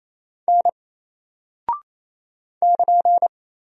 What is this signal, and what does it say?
Morse code, heard as a keyed tone, about 27 wpm, 700 Hz. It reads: N <KN>
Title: Single Character (Letter:Number:Punctuation:Prosign) 28wpm